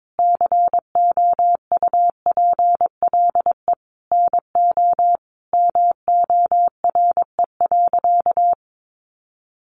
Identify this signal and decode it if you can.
Morse code, heard as a keyed tone, about 22 wpm, 700 Hz. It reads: COUPLE NO MORE.
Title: Sentences from Top 1100 Words 22wpm